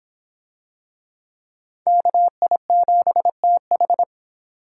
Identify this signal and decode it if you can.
Morse code, heard as a keyed tone, about 25 wpm, 700 Hz. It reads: KI7TH